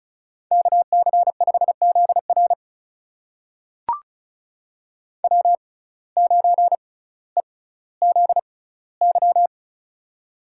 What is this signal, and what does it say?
Morse code, heard as a keyed tone, about 36 wpm, spaced as for 18 wpm, 700 Hz. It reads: KC5ZR W9EZY
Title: USA Call Signs - 5 Characters - 6x Character Spacing - 35wpm